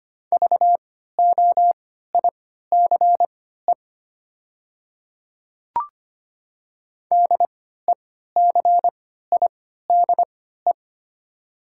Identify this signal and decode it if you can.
Morse code, heard as a keyed tone, about 25 wpm, spaced as for 14 wpm, 700 Hz. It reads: VOICE DECIDE
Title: 3x Character Spacing - Top 500 Words - 25wpm